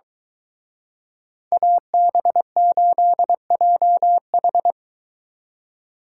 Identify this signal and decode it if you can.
Morse code, heard as a keyed tone, about 23 wpm, 700 Hz. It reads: AB8JH